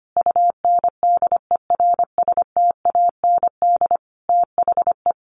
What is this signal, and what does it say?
Morse code, heard as a keyed tone, about 25 wpm, 700 Hz. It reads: UNDERSTAND THE
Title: Sentences from Top 1100 Words 25wpm